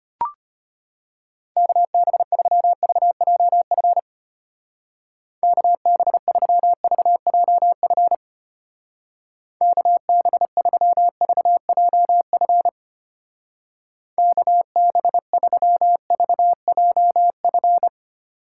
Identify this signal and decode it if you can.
Morse code, heard as a keyed tone, about 31 wpm, 700 Hz. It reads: KB3VJF KB3VJF KB3VJF KB3VJF